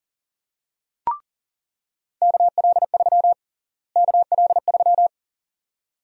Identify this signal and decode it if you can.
Morse code, heard as a keyed tone, about 40 wpm, 700 Hz. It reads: KL3 KL3
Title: USA Call Sign Prefixes - Repeated 2x - 40wpm